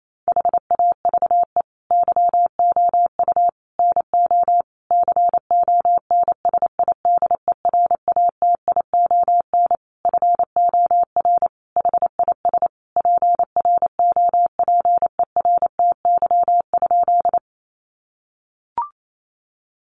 Morse code, heard as a keyed tone, about 28 wpm, 700 Hz. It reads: HAVE YOU NO CONSIDERATION FOR HIS PROPERTY?
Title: Sentences from Top 1500 Words 28wpm